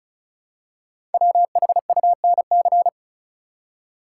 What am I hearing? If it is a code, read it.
Morse code, heard as a keyed tone, about 36 wpm, 700 Hz. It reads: WHUNC
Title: Code Groups - 5 Letters - 35wpm